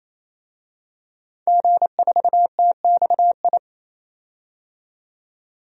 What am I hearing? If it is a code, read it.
Morse code, heard as a keyed tone, about 28 wpm, 700 Hz. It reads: G4TXI
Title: DXCC-100 Call Signs - 5 Characters - 28wpm